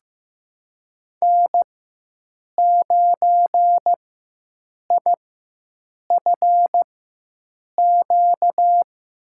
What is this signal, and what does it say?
Morse code, heard as a keyed tone, about 15 wpm, spaced as for 7 wpm, 700 Hz. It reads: N9IFQ
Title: USA Call Signs - 5 Characters - 4x Character Spacing - 15wpm